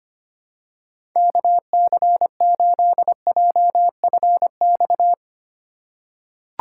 Morse code, heard as a keyed tone, about 25 wpm, 700 Hz. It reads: KC8JFX
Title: USA Call Signs - 6 Characters - 4x Character Spacing - 25wpm